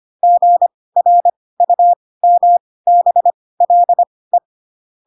Morse code, heard as a keyed tone, about 25 wpm, spaced as for 18 wpm, 700 Hz. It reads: GRUMBLE